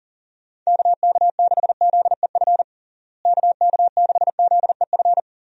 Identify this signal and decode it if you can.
Morse code, heard as a keyed tone, about 40 wpm, 700 Hz. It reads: KK6ZEF KK6ZEF
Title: USA Call Signs - 6 Characters - Repeated 3x - 40wpm